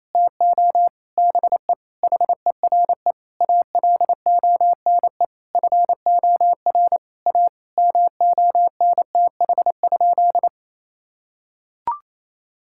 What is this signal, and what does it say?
Morse code, heard as a keyed tone, about 28 wpm, 700 Hz. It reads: TO BE HERE ALONE FOR A MONTH?
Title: Sentences from Top 1000 Words - Review Part 5 - 28wpm